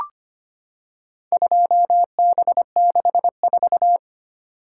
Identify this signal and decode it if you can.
Morse code, heard as a keyed tone, about 25 wpm, 700 Hz. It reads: 2B64
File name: practice / 25wpm / Four Letter-Number Combinations - 25wpm